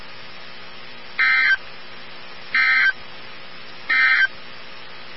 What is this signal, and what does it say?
Emergency Alert System (SAME) tones, heard as harsh buzzing bursts. It is heard from 1.1 to 4.4 s.
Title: NOAA Weather Radio 162.550